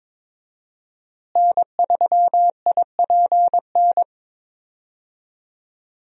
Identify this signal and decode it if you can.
Morse code, heard as a keyed tone, about 22 wpm, 700 Hz. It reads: N3IPN
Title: USA Call Signs - 5 Characters - Repeated 3x - 22wpm